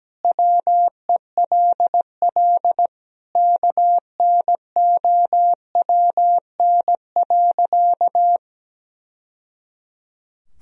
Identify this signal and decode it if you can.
Morse code, heard as a keyed tone, about 17 wpm, 700 Hz. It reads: WELL KNOWN.
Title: Sentences from Top 1000 Words - Review Part 4 - 17wpm